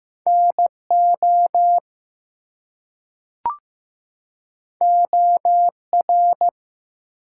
Morse code, heard as a keyed tone, about 16 wpm, 700 Hz. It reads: NO OR
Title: Two Letter Words - 15wpm